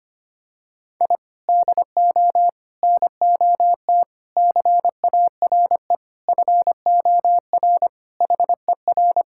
Morse code, heard as a keyed tone, about 25 wpm, 700 Hz. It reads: I DO NOT CARE FOR HER
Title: Sentences from Top 1000 Words - Review Part 6 - 25wpm